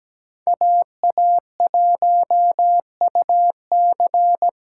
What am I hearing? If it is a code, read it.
Morse code, heard as a keyed tone, about 17 wpm, 700 Hz. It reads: AA1UC
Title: K1USN SST QSO Practice - 17wpm